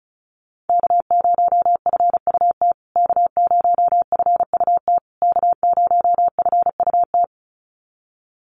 Morse code, heard as a keyed tone, about 34 wpm, 700 Hz. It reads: K0FUT K0FUT K0FUT
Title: USA Call Signs Repeated 3x 35wpm